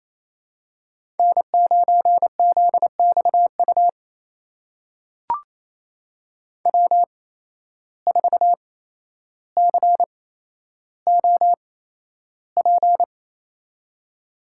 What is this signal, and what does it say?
Morse code, heard as a keyed tone, about 27 wpm, spaced as for 12 wpm, 700 Hz. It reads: N9ZXU W4COP
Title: USA Call Signs - 5 Characters - 8x Character Spacing - 28wpm